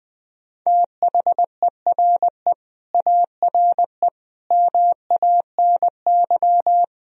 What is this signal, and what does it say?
Morse code, heard as a keyed tone, about 20 wpm, 700 Hz. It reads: THERE ARE MANY
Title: Sentences from Top 1000 Words - Review Part 1 - 20wpm